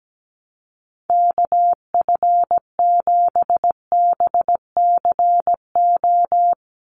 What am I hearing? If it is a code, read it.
Morse code, heard as a keyed tone, about 17 wpm, 700 Hz. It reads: KF7BCO
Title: USA Call Signs 17wpm